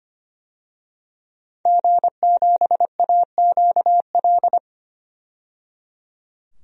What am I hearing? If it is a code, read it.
Morse code, heard as a keyed tone, about 25 wpm, 700 Hz. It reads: G7AQL
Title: DXCC-100 Call Signs - 5 Characters - 25wpm